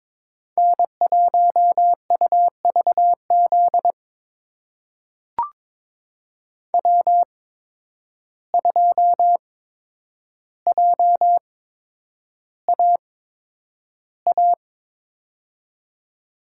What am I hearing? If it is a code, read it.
Morse code, heard as a keyed tone, about 22 wpm, spaced as for 9 wpm, 700 Hz. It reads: N1UVZ W2JAA
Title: USA Call Signs - 5 Characters - 8x Character Spacing - 22wpm